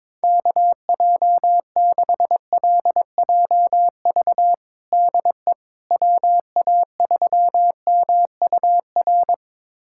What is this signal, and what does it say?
Morse code, heard as a keyed tone, about 22 wpm, 700 Hz. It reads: KJ6LJV DE WA3MUR
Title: USA Callsign DE Callsign - 6 Characters - 22wpm